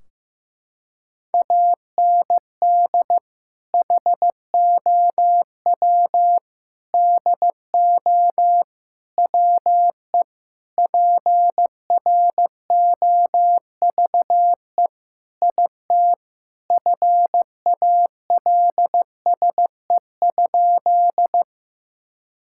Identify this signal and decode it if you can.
Morse code, heard as a keyed tone, about 15 wpm, 700 Hz. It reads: AND HOW DO WE PROVE IT FALSE?
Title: Sentences from Top 1600 Words 15wpm